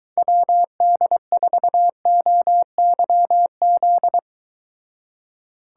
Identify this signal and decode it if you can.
Morse code, heard as a keyed tone, about 23 wpm, 700 Hz. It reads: WD4OYZ